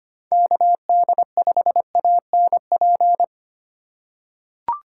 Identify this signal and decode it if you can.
Morse code, heard as a keyed tone, about 25 wpm, 700 Hz. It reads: KD5ANP
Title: USA Call Signs - 6 Characters - 3x Character Spacing - 25wpm